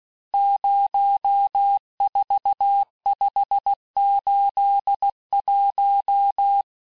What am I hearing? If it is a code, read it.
Morse code, heard as a keyed tone, about 16 wpm, 800 Hz. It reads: 04581
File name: cours-du-lundi files / cours-1592--teste-chiffre-vitesse-15mpm